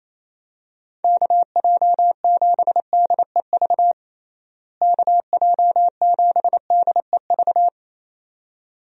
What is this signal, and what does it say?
Morse code, heard as a keyed tone, about 27 wpm, 700 Hz. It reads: KJ7DEV KJ7DEV
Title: USA Call Signs - 6 Characters - Repeated 2x - 28wpm